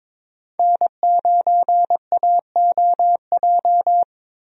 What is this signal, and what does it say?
Morse code, heard as a keyed tone, about 21 wpm, 700 Hz. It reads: N9AOJ
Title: USA Call Signs - 5 Characters - Repeated 3x - 22wpm